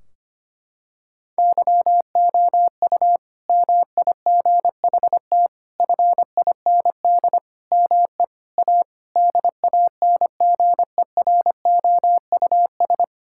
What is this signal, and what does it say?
Morse code, heard as a keyed tone, about 25 wpm, 700 Hz. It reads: YOU MIGHT FIND ME A DANGEROUS
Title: Sentences from Top 1600 Words 25wpm